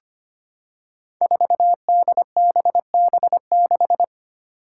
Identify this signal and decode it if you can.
Morse code, heard as a keyed tone, about 25 wpm, 700 Hz. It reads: 4DBB6
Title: Five Challenging Letter-Number Combinations 25wpm